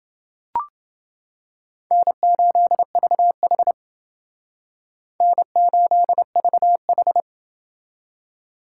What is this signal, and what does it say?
Morse code, heard as a keyed tone, about 29 wpm, 700 Hz. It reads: N8VH N8VH